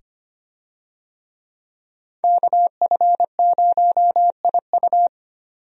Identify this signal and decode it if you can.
Morse code, heard as a keyed tone, about 25 wpm, 700 Hz. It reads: KF0IU
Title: USA Call Signs - 5 Characters - 25wpm